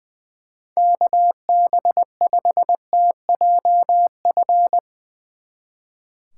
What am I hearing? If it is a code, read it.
Morse code, heard as a keyed tone, about 20 wpm, 700 Hz. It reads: KB5TJF
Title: Speed Racing - USA 2x3 Call Signs 20wpm